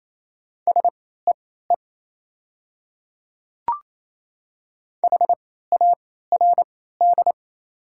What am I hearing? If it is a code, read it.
Morse code, heard as a keyed tone, about 27 wpm, spaced as for 16 wpm, 700 Hz. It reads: SEE HARD
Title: 3x Character Spacing - Top 300 Words - 28wpm